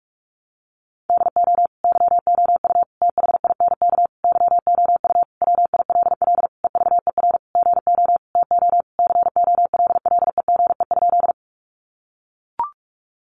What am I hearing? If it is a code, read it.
Morse code, heard as a keyed tone, about 44 wpm, 700 Hz. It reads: DO YOU THINK YOU WILL EVER GO TO COLLEGE?
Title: Sentences from Top 1300 Words 45wpm